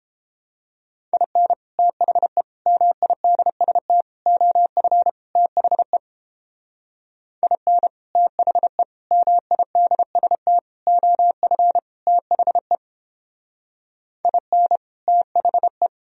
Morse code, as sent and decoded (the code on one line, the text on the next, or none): .. -. / - .... / -- .. -.. ... - / --- ..-. / - .... / .. -. / - .... / -- .. -.. ... - / --- ..-. / - .... / .. -. / - .... .
IN THE MIDST OF THE IN THE MIDST OF THE IN THE